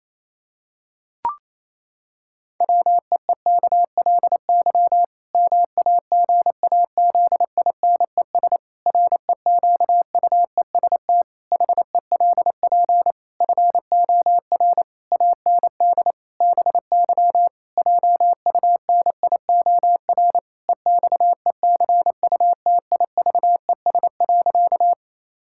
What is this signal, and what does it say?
Morse code, heard as a keyed tone, about 28 wpm, 700 Hz. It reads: WEEKLY MAGAZINES REQUEST HELP FOR AND BY JUNIOR EXECUTIVES.
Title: Pangram Challenge v2 - 28wpm